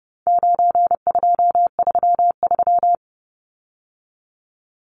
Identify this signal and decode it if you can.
Morse code, heard as a keyed tone, about 30 wpm, 700 Hz. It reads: 9233
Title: Numbers - 4-Digit 30wpm